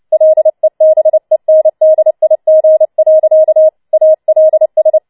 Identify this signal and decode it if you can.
Morse code, heard as a keyed tone, about 29 wpm, 630 Hz. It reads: LEBENDIG. ALS
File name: Schneewittchen Teil3 30wpm